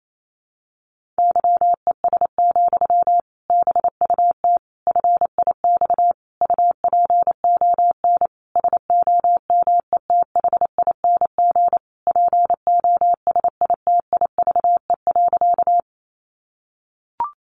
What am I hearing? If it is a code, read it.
Morse code, heard as a keyed tone, about 28 wpm, 700 Hz. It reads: YES, BUT FIX UPON SOMETHING POSITIVE.
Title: Sentences from Top 1400 Words 28wpm